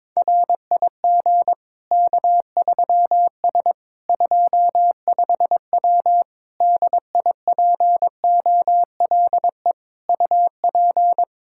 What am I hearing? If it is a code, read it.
Morse code, heard as a keyed tone, about 22 wpm, 700 Hz. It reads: RIG K3S 25W DIPOLE UP
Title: K1USN SST QSO Practice 22wpm